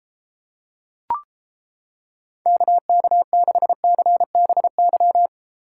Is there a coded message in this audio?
Yes: Morse code from 2.5 to 5.3 s, heard as a keyed tone, about 31 wpm, 700 Hz. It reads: KK6CBY